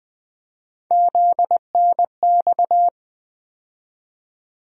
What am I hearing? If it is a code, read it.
Morse code, heard as a keyed tone, about 20 wpm, 700 Hz. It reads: ZNX
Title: USA Call Sign Postfixes - 3 Letters 20wpm